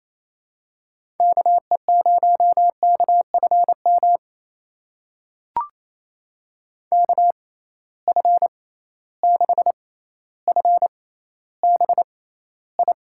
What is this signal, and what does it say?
Morse code, heard as a keyed tone, about 28 wpm, spaced as for 14 wpm, 700 Hz. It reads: KE0KFM KF6FBI
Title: USA Call Signs - 6 Characters - 6x Character Spacing - 28wpm